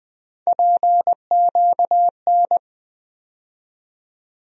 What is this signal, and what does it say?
Morse code, heard as a keyed tone, about 21 wpm, 700 Hz. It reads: PQN
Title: USA Call Sign Postfixes - 3 Letters 20wpm